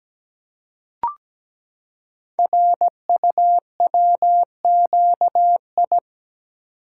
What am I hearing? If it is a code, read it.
Morse code, heard as a keyed tone, about 17 wpm, 700 Hz. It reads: RUWQI
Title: Code Groups - 5 Letters - 17wpm